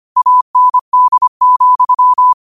Morse code, heard as a keyed tone, about 25 wpm, 1000 Hz. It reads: AND,